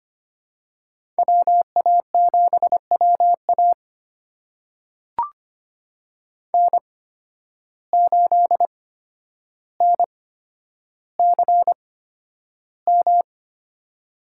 Morse code, heard as a keyed tone, about 25 wpm, spaced as for 11 wpm, 700 Hz. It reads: WA7WA N8NCM
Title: USA Call Signs - 5 Characters - 8x Character Spacing - 25wpm